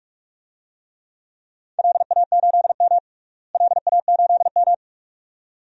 Morse code, heard as a keyed tone, about 43 wpm, 700 Hz. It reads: LA8M LA8M
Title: DXCC-100 Call Signs - 4 Characters - Repeated 2x - 45wpm